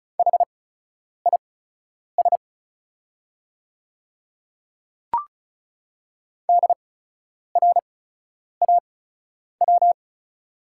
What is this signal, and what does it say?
Morse code, heard as a keyed tone, about 36 wpm, spaced as for 10 wpm, 690 Hz. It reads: HIS DRAW
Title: 8x Character Spacing - Top 300 Words - 35wpm